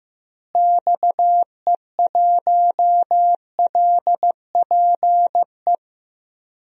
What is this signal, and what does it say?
Morse code, heard as a keyed tone, about 16 wpm, 700 Hz. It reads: XE1LPE
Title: DXCC-100 Call Signs - 6 Characters - Repeated 3x - 15wpm